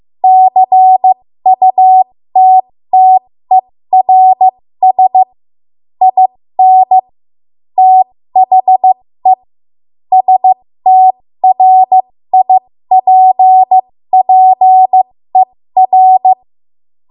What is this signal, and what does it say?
Morse code, heard as a keyed tone, about 15 wpm, spaced as for 13 wpm, 750 Hz. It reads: CUTTERS IN THE STRIPPER
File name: Morse / 13WPM